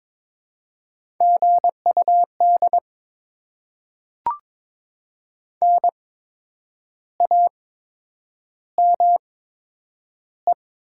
Morse code, heard as a keyed tone, about 22 wpm, spaced as for 8 wpm, 700 Hz. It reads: GUD NAME
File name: Top 20 QSO Elements - 8x Character Spacing - 22wpm